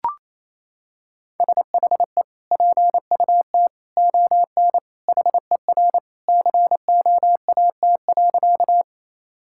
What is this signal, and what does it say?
Morse code, heard as a keyed tone, about 28 wpm, 700 Hz. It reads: SHE PUT ON HER COAT.
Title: Sentences from Top 1600 Words 28wpm